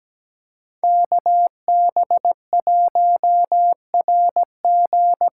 Morse code, heard as a keyed tone, about 17 wpm, 700 Hz. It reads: KB1RG
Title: USA Call Signs - 8x Character Spacing - 17wpm